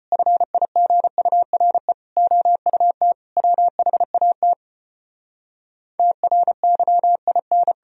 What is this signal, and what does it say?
Morse code, heard as a keyed tone, about 36 wpm, 700 Hz. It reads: FIGURE OUT WHAT TRYIN